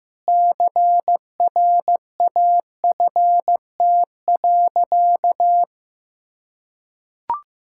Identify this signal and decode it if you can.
Morse code, heard as a keyed tone, about 15 wpm, 700 Hz. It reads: CRAFT.
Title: Sentences from Top 1000 Words - Review Part 6 - 15wpm